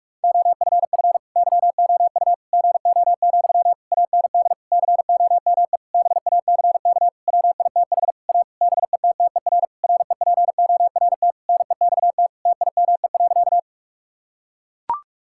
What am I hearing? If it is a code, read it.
Morse code, heard as a keyed tone, about 41 wpm, 700 Hz. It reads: OFF YOU GO, AND COME BACK WITH A BETTER REPORT NEXT TIME.
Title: Sentences from Top 1000 Words - Review Part 6 - 45wpm